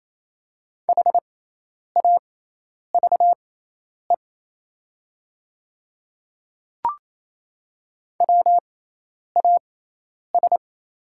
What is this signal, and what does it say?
Morse code, heard as a keyed tone, about 28 wpm, spaced as for 10 wpm, 690 Hz. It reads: HAVE WAS
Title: 6x Character Spacing - Top 200 Words - 28wpm